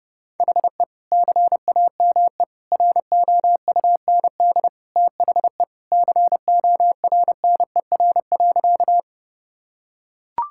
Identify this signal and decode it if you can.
Morse code, heard as a keyed tone, about 32 wpm, 700 Hz. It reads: HE CAME ROUND THE CORNER.
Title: Sentences from Top 1600 Words 30wpm